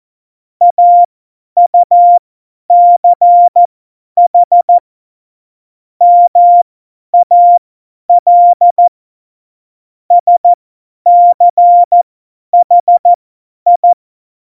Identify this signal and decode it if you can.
Morse code, heard as a keyed tone, about 14 wpm, spaced as for 10 wpm, 700 Hz. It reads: AUCH MAL SCHI
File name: Morsekurs / Witze / Witz 22. Mai 2018, 50 F70